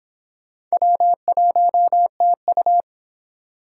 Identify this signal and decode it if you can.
Morse code, heard as a keyed tone, about 25 wpm, 700 Hz. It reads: W1TU